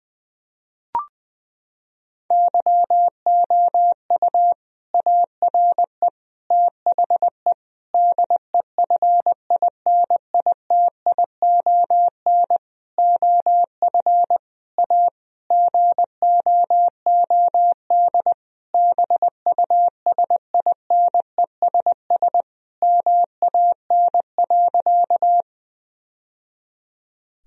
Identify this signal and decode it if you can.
Morse code, heard as a keyed tone, about 20 wpm, 700 Hz. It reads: YOU ARE THE DEFINITION OF A GOOD BUSINESS MAN.
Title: Sentences from Top 1000 Words - Review Part 4 - 20wpm